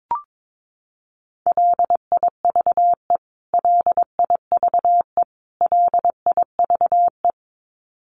Morse code, heard as a keyed tone, about 22 wpm, 700 Hz. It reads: LIVE LIVE LIVE
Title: Top 300 Words Repeated 22wpm